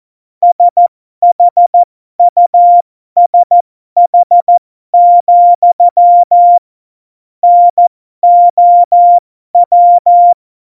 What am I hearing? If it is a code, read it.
Morse code, heard as a keyed tone, about 14 wpm, spaced as for 12 wpm, 700 Hz. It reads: SHUSH, NOW
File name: Morsekurs / Witze